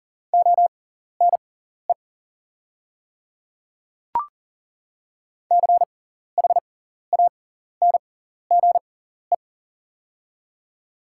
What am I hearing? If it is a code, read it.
Morse code, heard as a keyed tone, about 40 wpm, spaced as for 14 wpm, 700 Hz. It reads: ONE CHANGE
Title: 6x Character Spacing - Top 300 Words - 40wpm